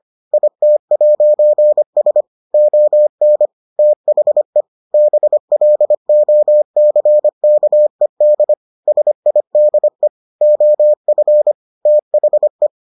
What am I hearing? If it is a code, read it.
Morse code, heard as a keyed tone, about 25 wpm, 600 Hz. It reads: IT'S ON THE BLOCKED SIDE OF THE